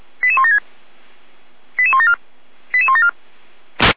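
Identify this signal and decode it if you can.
Five-tone selective call (ZVEI), heard as quick runs of five tones, tones of about 70 ms. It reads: 88176 88174 88174
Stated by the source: ZVEI five-tone call received by SDR, recorded by feuerwehr-orlamuende (fire brigade Orlamünde).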